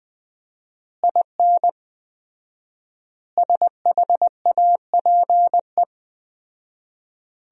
Morse code, heard as a keyed tone, about 20 wpm, 700 Hz. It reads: IN SHAPE
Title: 4x Word-Spacing - Sentences from Top 500 Words - 20wpm